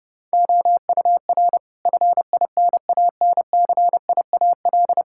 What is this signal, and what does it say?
Morse code, heard as a keyed tone, about 30 wpm, 700 Hz. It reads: OUR FINANCIAL
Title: Sentences from Top 1000 Words - Review Part 5 - 30wpm